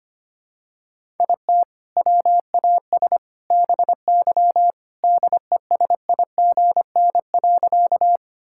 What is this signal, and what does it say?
Morse code, heard as a keyed tone, about 25 wpm, 700 Hz. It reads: IT WAS BY DESIGN.
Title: Sentences from Top 1000 Words - Review Part 5 - 25wpm